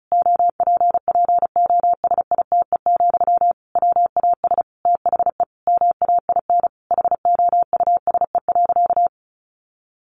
Morse code, heard as a keyed tone, about 34 wpm, 700 Hz. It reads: OPPOSITE, WAS THE MAIN HOUSE.
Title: Sentences from Top 1300 Words 35wpm